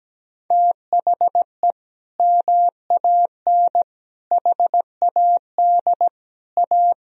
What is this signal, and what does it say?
Morse code, heard as a keyed tone, about 17 wpm, 700 Hz. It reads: THE MAN HAD A
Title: Sentences from Top 1000 Words - Review Part 2 - 17wpm